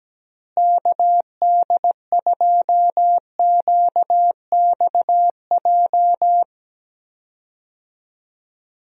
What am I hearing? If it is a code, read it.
Morse code, heard as a keyed tone, about 17 wpm, 700 Hz. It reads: KD2QXJ
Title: USA Call Signs - 6 Characters - Repeated 2x - 17wpm